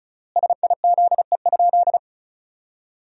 Morse code, heard as a keyed tone, about 34 wpm, 700 Hz. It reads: SIZE?
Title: 3x Word-Spacing - Sentences from Top 500 Words - 35wpm